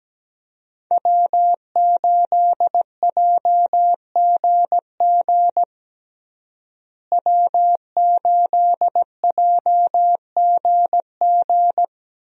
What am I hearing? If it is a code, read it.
Morse code, heard as a keyed tone, about 17 wpm, 700 Hz. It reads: W8JGG W8JGG
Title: USA Call Signs - 5 Characters - Repeated 3x - 17wpm